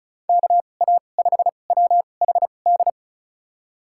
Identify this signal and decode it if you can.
Morse code, heard as a keyed tone, about 36 wpm, spaced as for 25 wpm, 700 Hz. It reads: KA5WHD
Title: USA Call Signs - 6 Characters - 2x Character Spacing - 35wpm